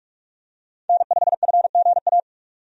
Morse code, heard as a keyed tone, about 46 wpm, 700 Hz. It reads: N5FGA
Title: USA Call Signs - 5 Characters - 45wpm